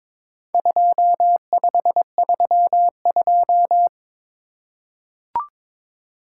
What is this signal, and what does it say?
Morse code, heard as a keyed tone, about 22 wpm, 700 Hz. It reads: 2532
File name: practice / 22wpm / Four Letter-Number Combinations - 22wpm